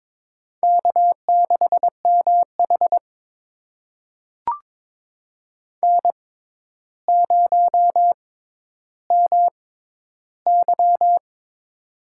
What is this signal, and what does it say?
Morse code, heard as a keyed tone, about 22 wpm, spaced as for 11 wpm, 700 Hz. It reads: K6MH N0MY
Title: USA Call Signs - 4 Characters - 6x Character Spacing - 22wpm